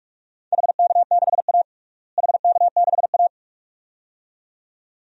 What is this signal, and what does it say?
Morse code, heard as a keyed tone, about 47 wpm, 700 Hz. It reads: HK6A HK6A
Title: DXCC-100 Call Signs - 4 Characters - Repeated 2x - 45wpm